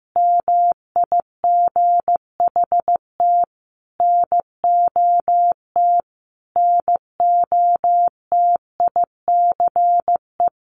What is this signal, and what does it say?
Morse code, heard as a keyed tone, about 15 wpm, 700 Hz. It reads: MIGHT NOT NOTICE
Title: Sentences from Top 1100 Words 15wpm